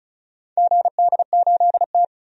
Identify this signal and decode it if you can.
Morse code, heard as a keyed tone, about 36 wpm, 700 Hz. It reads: GD8T
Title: DXCC-100 Call Signs - 4 Characters - 35wpm